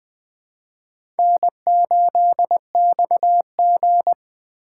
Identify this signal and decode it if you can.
Morse code, heard as a keyed tone, about 20 wpm, 700 Hz. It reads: N8XG
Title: USA Call Signs - 4x Character Spacing - 20wpm